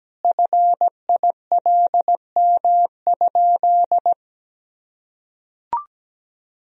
Morse code, heard as a keyed tone, about 17 wpm, 700 Hz. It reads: FILM?
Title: Sentences from Top 1000 Words - Review Part 3 - 17wpm